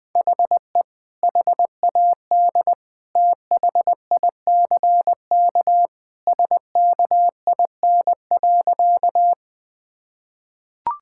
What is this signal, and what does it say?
Morse code, heard as a keyed tone, about 21 wpm, 700 Hz. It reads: HE HAD THICK SKIN.
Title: Sentences from Top 1600 Words 20wpm